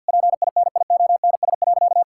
Morse code, heard as a keyed tone, about 50 wpm, 700 Hz. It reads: PINIONS.
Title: Pangram Challenge v2 - 50wpm